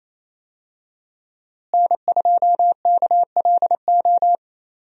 Morse code, heard as a keyed tone, about 27 wpm, 700 Hz. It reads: N2KLO
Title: USA Call Signs - 5 Characters - Repeated 3x - 28wpm